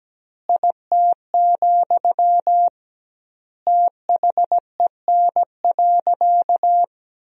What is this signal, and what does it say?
Morse code, heard as a keyed tone, about 17 wpm, 700 Hz. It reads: IT, THEN.
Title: 2x Word-Spacing - Sentences from Top 200 Words - 17wpm